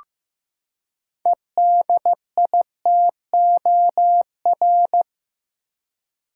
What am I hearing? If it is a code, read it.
Morse code, heard as a keyed tone, about 15 wpm, 700 Hz. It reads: EDITOR
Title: Six Letter Words - 15wpm